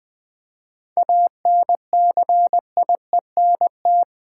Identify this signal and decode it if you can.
Morse code, heard as a keyed tone, about 20 wpm, 700 Hz. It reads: ANCIENT
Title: 7 Letter Words - First 3 Letters Given - 20wpm